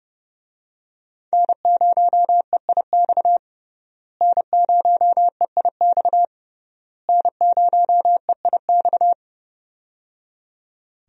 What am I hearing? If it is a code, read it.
Morse code, heard as a keyed tone, about 30 wpm, 700 Hz. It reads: N0EIX N0EIX N0EIX
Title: USA Call Signs - 5 Characters - Repeated 3x - 30wpm